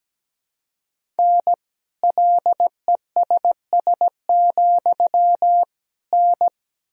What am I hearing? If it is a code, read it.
Morse code, heard as a keyed tone, about 17 wpm, 700 Hz. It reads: N LESS, N
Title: Sentences from Top 400 Words and Abbreviations 17wpm